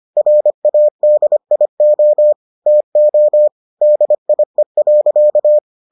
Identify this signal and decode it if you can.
Morse code, heard as a keyed tone, about 25 wpm, 600 Hz. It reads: RADIO TO DIE.